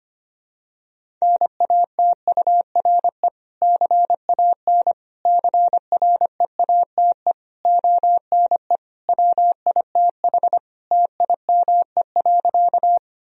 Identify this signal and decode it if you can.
Morse code, heard as a keyed tone, about 25 wpm, 700 Hz. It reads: NATURE CAN CREATE ONE WITH TIME.
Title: Sentences from Top 1000 Words - Review Part 3 - 25wpm